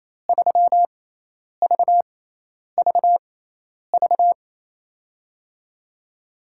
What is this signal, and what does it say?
Morse code, heard as a keyed tone, about 28 wpm, spaced as for 10 wpm, 700 Hz. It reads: 3VVV
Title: USA Call Signs - 5 Characters - 6x Character Spacing - 28wpm